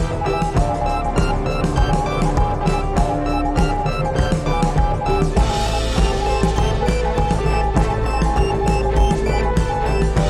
Morse code, heard as a keyed tone, about 20 wpm, 820 Hz. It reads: ARE ALIVE OR DEAD.IT